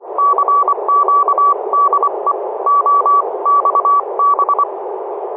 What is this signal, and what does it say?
Morse code, heard as a keyed tone, about 25 wpm, spaced as for 22 wpm, 1140 Hz, with noise. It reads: CQ DE OXB
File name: oxb-tfc-list